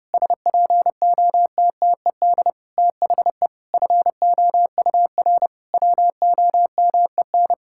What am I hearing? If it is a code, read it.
Morse code, heard as a keyed tone, about 30 wpm, 700 Hz. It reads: SPOTTED THE FOUR WOMEN
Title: Pangram Challenge v2 - 30wpm